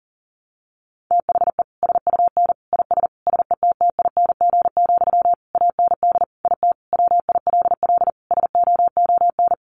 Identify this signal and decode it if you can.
Morse code, heard as a keyed tone, about 40 wpm, 700 Hz. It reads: THE SUN IS SETTING, AND IT WILL SOON
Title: Sentences from Top 1300 Words 40wpm